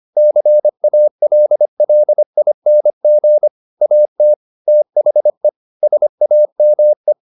Morse code, heard as a keyed tone, about 25 wpm, 600 Hz. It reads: CALLING AT THE SAME